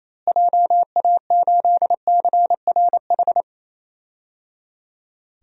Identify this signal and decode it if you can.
Morse code, heard as a keyed tone, about 28 wpm, 700 Hz. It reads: JA8CRH
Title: DXCC-100 Call Signs - 6 Characters - Repeated 3x - 28wpm